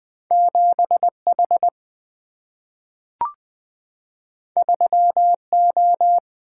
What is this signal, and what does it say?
Morse code, heard as a keyed tone, about 20 wpm, 700 Hz. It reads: 7 H 3 O
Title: Single-Letter Single-Number 20wpm